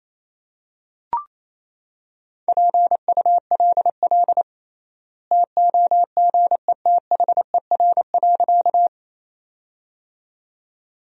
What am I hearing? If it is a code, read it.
Morse code, heard as a keyed tone, about 28 wpm, 700 Hz. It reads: PULL TOGETHER.
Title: 3x Word-Spacing - Sentences from Top 400 Words - 28wpm